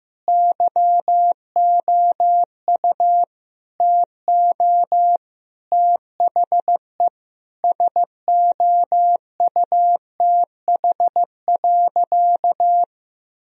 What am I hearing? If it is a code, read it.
Morse code, heard as a keyed tone, about 16 wpm, 700 Hz. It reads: YOU TO THE SOUTH.
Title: Sentences from Top 1000 Words - Review Part 2 - 15wpm